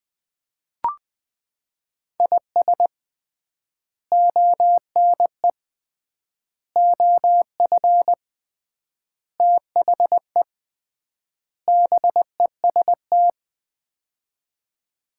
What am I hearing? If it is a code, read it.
Morse code, heard as a keyed tone, about 20 wpm, 700 Hz. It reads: IS ONE OF THE BEST
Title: 3x Word-Spacing - Sets of 5 Words - 20wpm